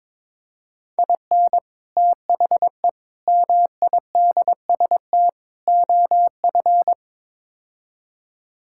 Morse code, heard as a keyed tone, about 22 wpm, 700 Hz. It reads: IN THE MIDST OF
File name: practice / 22wpm / Build A Phrase - 5-Word Phrases - 22wpm